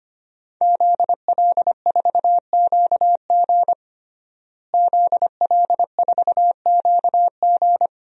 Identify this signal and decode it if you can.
Morse code, heard as a keyed tone, about 25 wpm, 700 Hz. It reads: ZL4QG ZL4QG
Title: DXCC-100 Call Signs - 5 Characters - Repeated 2x - 25wpm